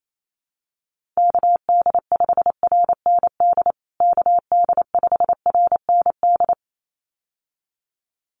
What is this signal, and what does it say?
Morse code, heard as a keyed tone, about 28 wpm, 700 Hz. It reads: KD5RND KD5RND
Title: USA Call Signs Repeated 2x 28wpm